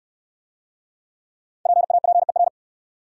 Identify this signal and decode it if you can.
Morse code, heard as a keyed tone, about 64 wpm, 700 Hz. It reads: PAPER